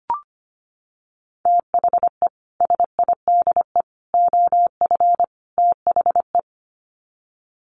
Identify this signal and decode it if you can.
Morse code, heard as a keyed tone, about 25 wpm, 700 Hz. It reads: THE SIDE OF THE
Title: Sets of 4 Words 25wpm